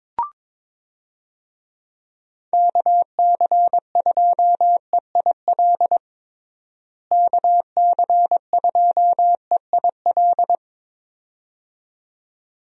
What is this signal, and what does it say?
Morse code, heard as a keyed tone, about 22 wpm, 700 Hz. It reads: KC2EIL KC2EIL
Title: USA Call Signs - 6 Characters - Repeated 2x - 22wpm